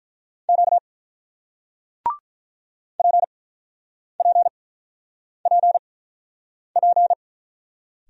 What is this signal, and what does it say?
Morse code, heard as a keyed tone, about 48 wpm, spaced as for 9 wpm, 700 Hz. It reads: Q PPPP